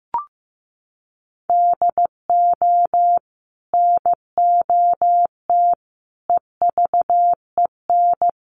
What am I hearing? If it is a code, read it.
Morse code, heard as a keyed tone, about 15 wpm, 700 Hz. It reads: DO NOT EVEN
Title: Sets of 4 Words 15wpm